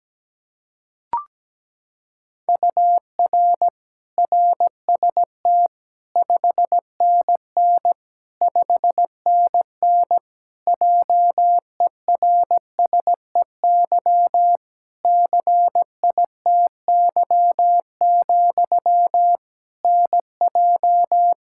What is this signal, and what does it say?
Morse code, heard as a keyed tone, about 17 wpm, 700 Hz. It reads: UR RST 5NN 5NN JERSEY CITY, NJ
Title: Practice QSO - Exchange 1 - Beginning - 17wpm